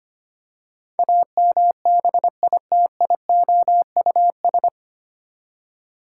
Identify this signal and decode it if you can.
Morse code, heard as a keyed tone, about 25 wpm, 700 Hz. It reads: AMBITIOUS
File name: Nine Letter Words - 25wpm